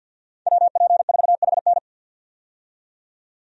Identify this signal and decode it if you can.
Morse code, heard as a keyed tone, about 51 wpm, 700 Hz. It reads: WP4HN
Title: USA Call Signs - 5 Characters - 50wpm